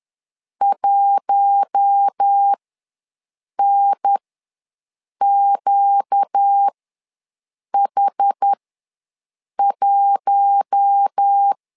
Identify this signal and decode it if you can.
Morse code, heard as a keyed tone, about 10 wpm, spaced as for 6 wpm, 800 Hz. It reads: 1NQH1
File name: LEZIONI / 4 Q1 ws 10wpm 2538